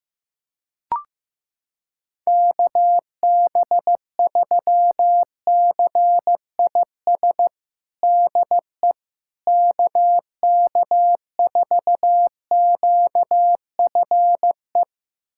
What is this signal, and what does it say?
Morse code, heard as a keyed tone, about 15 wpm, 700 Hz. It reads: KB3CIS DE KK4QFE
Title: USA Callsign DE Callsign - 6 Characters - 15wpm